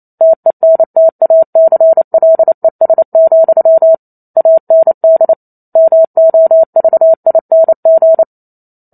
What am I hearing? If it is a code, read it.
Morse code, heard as a keyed tone, about 28 wpm, 650 Hz. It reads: TENTACLES, AND MOVING
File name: Chapter 2 - 28 WPM